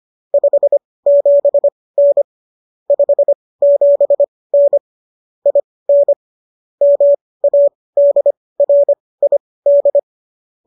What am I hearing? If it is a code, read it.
Morse code, heard as a keyed tone, about 25 wpm, spaced as for 18 wpm, 570 Hz. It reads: 57N 57N IN MADRID